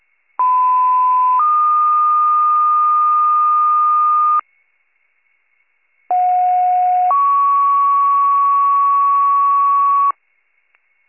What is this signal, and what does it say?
Two-tone page, heard as a one-second tone, then a three-second tone, about 1010 then 1220 Hz, about 730 then 1120 Hz. It is heard from 0.4 to 10.1 s.